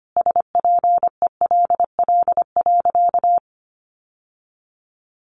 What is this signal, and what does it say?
Morse code, heard as a keyed tone, about 25 wpm, 700 Hz. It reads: SPELL.
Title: Sentences from Top 1200 Words 25wpm